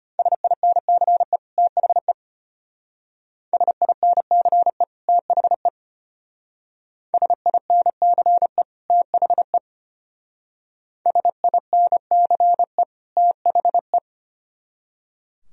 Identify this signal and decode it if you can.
Morse code, heard as a keyed tone, about 33 wpm, 700 Hz. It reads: SINCE THE SINCE THE SINCE THE SINCE THE